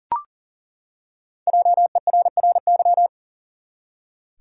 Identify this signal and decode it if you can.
Morse code, heard as a keyed tone, about 38 wpm, 700 Hz. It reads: JERRY
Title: USA Common Names - Max 5 Letters - 40wpm